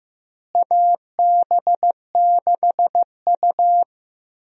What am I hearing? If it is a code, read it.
Morse code, heard as a keyed tone, about 16 wpm, 700 Hz. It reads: AB6U
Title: USA Call Signs - 4 Characters - Repeated 3x - 15wpm